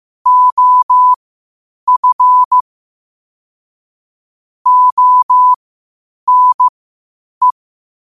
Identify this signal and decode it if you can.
Morse code, heard as a keyed tone, about 15 wpm, spaced as for 8 wpm, 1000 Hz. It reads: OF ONE
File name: qotdfiles / 8wpm